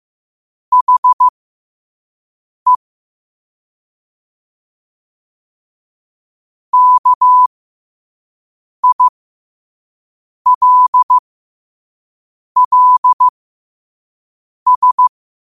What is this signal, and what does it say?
Morse code, heard as a keyed tone, about 15 wpm, spaced as for 5 wpm, 1000 Hz. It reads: HE KILLS
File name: qotdfiles / 5wpm